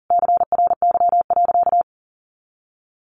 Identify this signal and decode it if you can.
Morse code, heard as a keyed tone, about 40 wpm, 700 Hz. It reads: CRY.
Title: Sentences from Top 1500 Words 40wpm